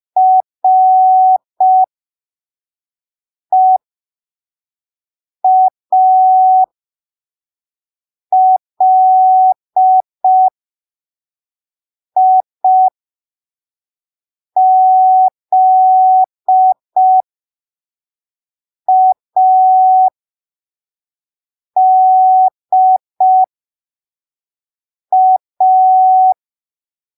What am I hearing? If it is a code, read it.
Morse code, heard as a keyed tone, about 5 wpm, spaced as for 3 wpm, 750 Hz. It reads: REALIZADA
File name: arquivos / texto / texto 07 05ppm